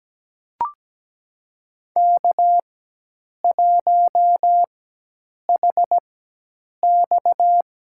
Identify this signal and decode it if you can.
Morse code, heard as a keyed tone, about 17 wpm, spaced as for 8 wpm, 700 Hz. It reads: K1HX
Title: USA Call Signs - 4 Characters - 4x Character Spacing - 17wpm